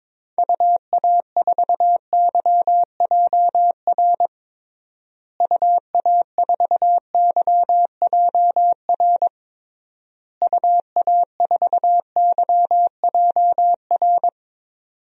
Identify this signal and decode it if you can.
Morse code, heard as a keyed tone, about 22 wpm, 700 Hz. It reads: UA4YJR UA4YJR UA4YJR
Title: DXCC-100 Call Signs - 6 Characters - Repeated 3x - 22wpm